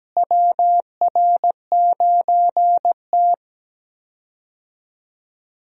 Morse code, heard as a keyed tone, about 17 wpm, 700 Hz. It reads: WR9T
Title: USA Call Signs - 4 Characters - Repeated 2x - 17wpm